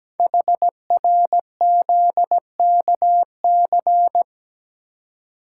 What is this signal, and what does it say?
Morse code, heard as a keyed tone, about 17 wpm, 700 Hz. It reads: HRZKC
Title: Code Groups - 5 Letters - 17wpm